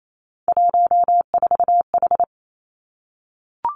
Morse code, heard as a keyed tone, about 28 wpm, 700 Hz. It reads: 14H
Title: Three Challenging Letter-Number Combinations 28wpm